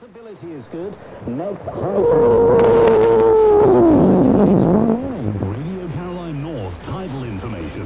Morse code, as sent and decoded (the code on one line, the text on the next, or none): none